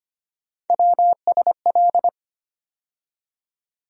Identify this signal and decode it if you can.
Morse code, heard as a keyed tone, about 25 wpm, 700 Hz. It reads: WSL
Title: USA Call Sign Postfixes - 3 Letters 25wpm